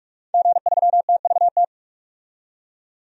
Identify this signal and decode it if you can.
Morse code, heard as a keyed tone, about 46 wpm, 700 Hz. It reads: G3TVT
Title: DXCC-100 Call Signs - 5 Characters - 45wpm